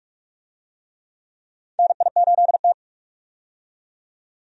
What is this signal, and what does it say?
Morse code, heard as a keyed tone, about 47 wpm, 700 Hz. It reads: NI8T
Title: USA Call Signs - 4 Characters - 45wpm